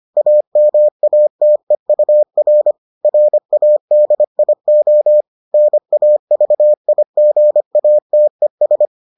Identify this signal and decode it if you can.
Morse code, heard as a keyed tone, about 25 wpm, 600 Hz. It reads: AMATEUR RADIO NAVIGATES